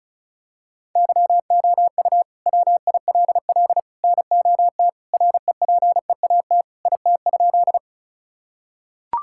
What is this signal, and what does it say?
Morse code, heard as a keyed tone, about 36 wpm, 700 Hz. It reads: YOU WILL NOT REPEAT IT?
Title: Sentences from Top 1600 Words 35wpm